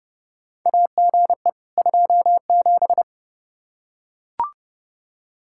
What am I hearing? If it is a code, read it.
Morse code, heard as a keyed tone, about 30 wpm, 700 Hz. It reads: AGE 27
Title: K1USN SST QSO Practice 30wpm